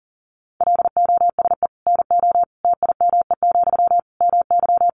Morse code, heard as a keyed tone, about 40 wpm, 700 Hz. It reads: LOSE NO TIME, MY
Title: Sentences from Top 1100 Words 40wpm